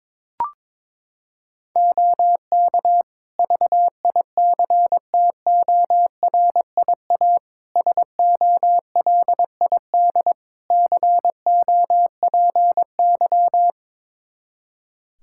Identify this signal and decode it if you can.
Morse code, heard as a keyed tone, about 21 wpm, 700 Hz. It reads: OK VICTORIA SOLID COPY
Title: Practice QSO - Exchange 3 - More Info - 22wpm